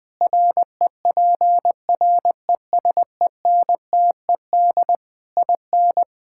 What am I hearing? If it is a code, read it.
Morse code, heard as a keyed tone, about 21 wpm, 700 Hz. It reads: REPRESENTED IN
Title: Sentences with 5-Word Phrases - 20wpm